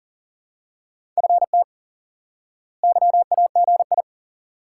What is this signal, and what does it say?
Morse code, heard as a keyed tone, about 40 wpm, 700 Hz. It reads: FT YAGI
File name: IQE Course 3 - Lesson 89 - YAGI - Two Elements - 40wpm